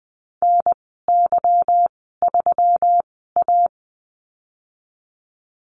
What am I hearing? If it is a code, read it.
Morse code, heard as a keyed tone, about 20 wpm, spaced as for 15 wpm, 700 Hz. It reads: NY3A
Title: 147 Lesson 24g - Letter y - Call Signs 20wpm